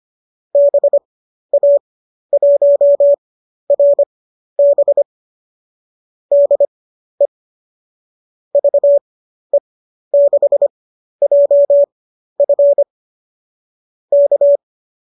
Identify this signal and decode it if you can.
Morse code, heard as a keyed tone, about 25 wpm, spaced as for 12 wpm, 570 Hz. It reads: BA1RB DE VE6JF K